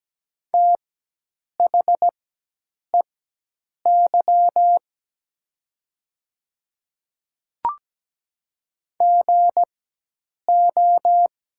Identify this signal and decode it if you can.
Morse code, heard as a keyed tone, about 17 wpm, spaced as for 8 wpm, 700 Hz. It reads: THEY GO
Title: 4x Character Spacing - Top 300 Words - 17wpm